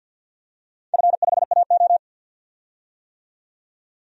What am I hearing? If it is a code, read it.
Morse code, heard as a keyed tone, about 51 wpm, 700 Hz. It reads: F5AO
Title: DXCC-100 Call Signs - 4 Characters - 50wpm